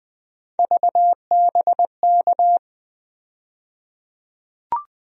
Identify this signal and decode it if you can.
Morse code, heard as a keyed tone, about 20 wpm, 700 Hz. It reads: VBK
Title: USA Call Sign Postfixes - 3 Letters 20wpm